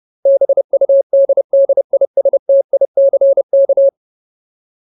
Morse code, heard as a keyed tone, about 30 wpm, 550 Hz. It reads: BUDDISTICK
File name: Ultimate Word List / 30WPM Ultimate